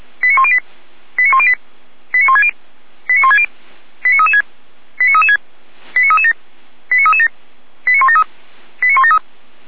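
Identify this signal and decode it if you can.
Five-tone selective call (ZVEI), heard as quick runs of five tones, tones of about 70 ms. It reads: 89108 89108 89160 89160 89336 89336 89337 89337 89173 89173
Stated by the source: Five-tone ZVEI selective call received on an SDR by feuerwehr-orlamuende.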